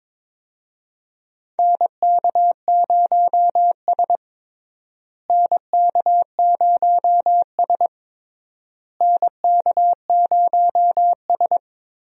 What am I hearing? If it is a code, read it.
Morse code, heard as a keyed tone, about 22 wpm, 700 Hz. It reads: NK0S NK0S NK0S
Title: USA Call Signs - 4 Characters - Repeated 3x - 22wpm